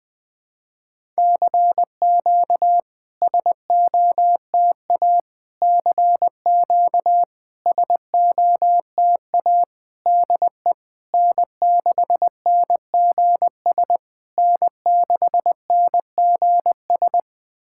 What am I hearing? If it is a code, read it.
Morse code, heard as a keyed tone, about 20 wpm, 700 Hz. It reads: CQ SOTA CQ SOTA DE N6NGS N6NGS
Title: SOTA QSO Practice 20wpm